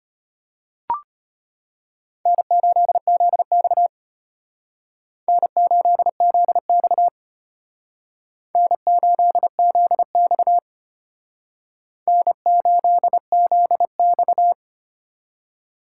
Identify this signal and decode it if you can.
Morse code, heard as a keyed tone, about 32 wpm, 700 Hz. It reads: N8ZX N8ZX N8ZX N8ZX